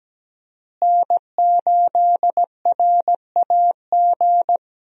Morse code, heard as a keyed tone, about 17 wpm, 700 Hz. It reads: N8RAG
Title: USA Call Signs - 5 Characters - Repeated 2x - 17wpm